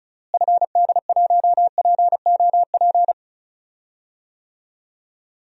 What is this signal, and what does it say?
Morse code, heard as a keyed tone, about 36 wpm, 700 Hz. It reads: FD1POP
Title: DXCC-100 Call Signs - 6 Characters - Repeated 2x - 35wpm